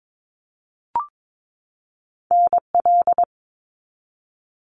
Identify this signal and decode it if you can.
Morse code, heard as a keyed tone, about 22 wpm, 700 Hz. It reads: NL
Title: Canadian Provinces and Territories 22wpm